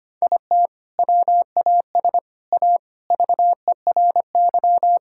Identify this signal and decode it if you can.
Morse code, heard as a keyed tone, about 25 wpm, 700 Hz. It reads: IT WAS A VERY
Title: Sentences from Top 1000 Words - Review Part 5 - 25wpm